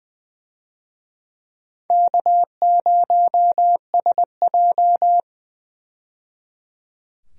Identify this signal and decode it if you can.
Morse code, heard as a keyed tone, about 20 wpm, 700 Hz. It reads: K0SJ
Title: USA Call Signs - 4 Characters - 20wpm